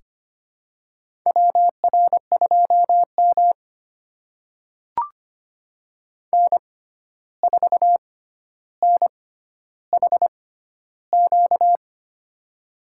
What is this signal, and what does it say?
Morse code, heard as a keyed tone, about 25 wpm, spaced as for 12 wpm, 700 Hz. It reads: WR2M N4NHQ
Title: USA Call Signs - 6x Character Spacing - 25wpm